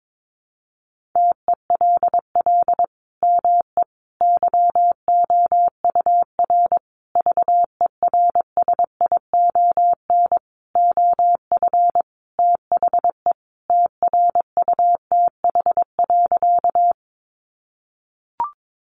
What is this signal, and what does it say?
Morse code, heard as a keyed tone, about 22 wpm, 700 Hz. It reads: TELL ME YOUR VERSION OF THE TRUTH.
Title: Sentences from Top 1400 Words 22wpm